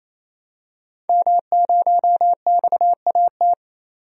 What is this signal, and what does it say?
Morse code, heard as a keyed tone, about 28 wpm, 690 Hz. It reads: M0XAT
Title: DXCC-100 Call Signs - 5 Characters - 28wpm